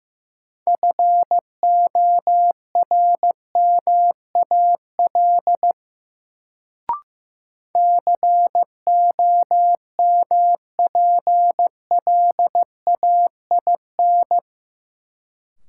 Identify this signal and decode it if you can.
Morse code, heard as a keyed tone, about 15 wpm, 700 Hz. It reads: FORMAL COMPLAIN
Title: Sentences from Top 1300 Words - New Word Drill - 15wpm